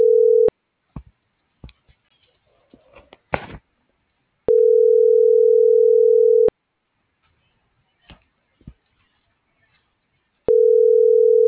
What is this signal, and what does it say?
Ringback tone (North America), 440 + 480 Hz.